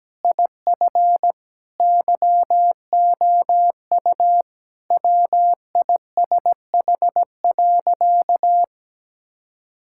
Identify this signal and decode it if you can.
Morse code, heard as a keyed tone, about 17 wpm, 700 Hz. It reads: IF YOU WISH.
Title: Sentences from Top 1600 Words 17wpm